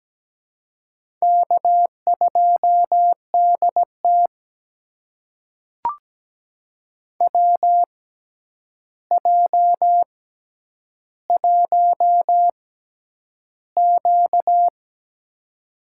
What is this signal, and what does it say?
Morse code, heard as a keyed tone, about 17 wpm, spaced as for 9 wpm, 700 Hz. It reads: K2DT WJ1Q
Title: USA Call Signs - 4 Characters - 6x Character Spacing - 17wpm